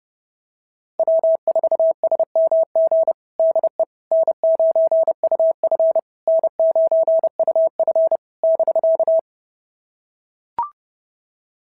Morse code, heard as a keyed tone, about 30 wpm, 650 Hz. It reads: W4SMG DE N9UF N9UF <BK>
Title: SOTA QSO Practice 30wpm